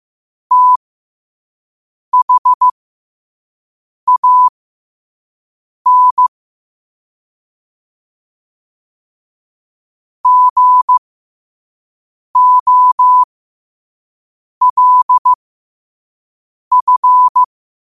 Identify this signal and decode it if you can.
Morse code, heard as a keyed tone, about 15 wpm, spaced as for 5 wpm, 1000 Hz. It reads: THAN GOLF
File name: qotdfiles / 5wpm